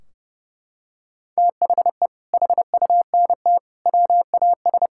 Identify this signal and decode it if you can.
Morse code, heard as a keyed tone, about 30 wpm, 700 Hz. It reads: THE HUNT WAS
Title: Sentences from Top 1600 Words 30wpm